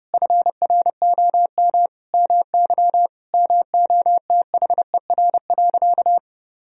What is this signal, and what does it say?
Morse code, heard as a keyed tone, about 32 wpm, 700 Hz. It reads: FROM MY MOTHER.
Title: Sentences from Top 1000 Words - Review Part 4 - 30wpm